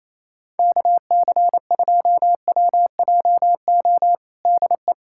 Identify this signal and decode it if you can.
Morse code, heard as a keyed tone, about 28 wpm, 700 Hz. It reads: KC2WJO DE
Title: USA Callsign DE Callsign 28wpm